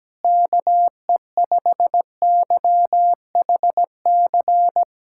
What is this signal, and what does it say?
Morse code, heard as a keyed tone, about 17 wpm, 700 Hz. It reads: KE5YHC
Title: USA Callsign DE Callsign 17wpm